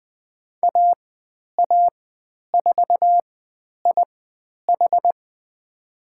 Morse code, heard as a keyed tone, about 20 wpm, spaced as for 10 wpm, 700 Hz. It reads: AA4IH
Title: 109 Lesson 18g - Letter m - Call Signs 20wpm